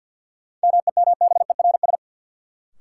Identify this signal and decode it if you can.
Morse code, heard as a keyed tone, about 53 wpm, 700 Hz. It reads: MEMBERS
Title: Sets of 3 Words - New Word Drill - 50wpm